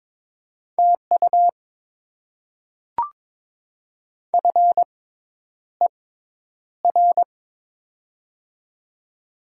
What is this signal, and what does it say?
Morse code, heard as a keyed tone, about 22 wpm, spaced as for 10 wpm, 700 Hz. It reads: TU FER
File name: Top 20 QSO Elements - 6x Character Spacing - 22wpm